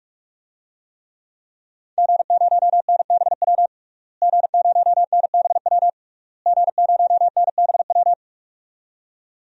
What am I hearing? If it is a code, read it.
Morse code, heard as a keyed tone, about 42 wpm, 700 Hz. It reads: G0NBW G0NBW G0NBW
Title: DXCC-100 Call Signs - 5 Characters - Repeated 3x - 45wpm